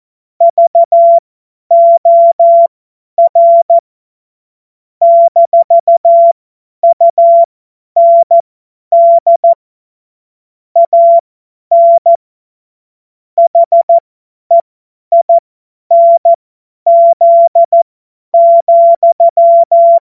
Morse code, heard as a keyed tone, about 14 wpm, spaced as for 10 wpm, 680 Hz. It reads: VOR - UND AN HEINZ,
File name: Morsekurs / Witze / Witz 17. Mai 2022 50 F70